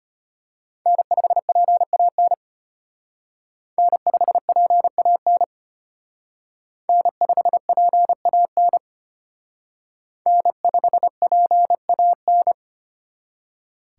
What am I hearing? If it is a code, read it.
Morse code, heard as a keyed tone, about 32 wpm, 700 Hz. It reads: N5PAN N5PAN N5PAN N5PAN